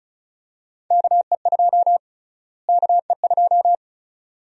Morse code, heard as a keyed tone, about 36 wpm, 700 Hz. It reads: KE2 KE2
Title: USA Call Sign Prefixes - Repeated 2x - 35wpm